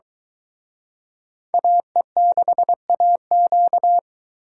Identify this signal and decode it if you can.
Morse code, heard as a keyed tone, about 23 wpm, 700 Hz. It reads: AE6AQ